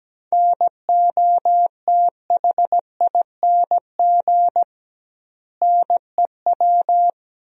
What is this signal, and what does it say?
Morse code, heard as a keyed tone, about 17 wpm, 700 Hz. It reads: NOTHING NEW
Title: 2x Word-Spacing - Sentences from Top 500 Words - 17wpm